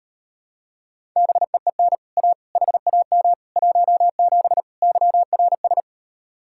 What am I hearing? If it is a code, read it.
Morse code, heard as a keyed tone, about 38 wpm, 700 Hz. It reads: BEEN A HAM 17 YRS